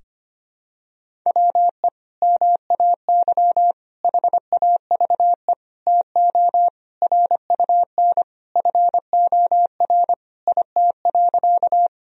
Morse code, heard as a keyed tone, about 25 wpm, 700 Hz. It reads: WE MAY HAVE TO RUN FOR IT.
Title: Sentences from Top 1000 Words - Review Part 1 - 25wpm